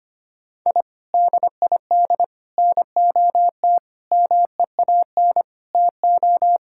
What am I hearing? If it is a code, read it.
Morse code, heard as a keyed tone, about 25 wpm, 700 Hz. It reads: I DID NOT MEAN TO
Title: Sentences from Top 1000 Words - Review Part 6 - 25wpm